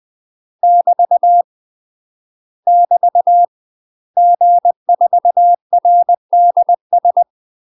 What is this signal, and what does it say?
Morse code, heard as a keyed tone, about 20 wpm, 700 Hz. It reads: <BT> <BT> G4RDS